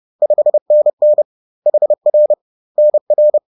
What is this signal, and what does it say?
Morse code, heard as a keyed tone, about 30 wpm, 600 Hz. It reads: 5NN HR NR
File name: 30wpm-4